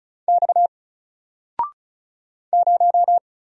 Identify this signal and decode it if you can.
Morse code, heard as a keyed tone, about 36 wpm, 700 Hz. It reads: X 0
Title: Single Letter-Number 35wpm